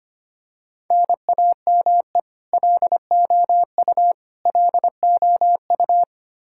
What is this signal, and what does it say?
Morse code, heard as a keyed tone, about 25 wpm, 700 Hz. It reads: NAME LOU LOU
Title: Practice QSO - Exchange 1 - Beginning - 25wpm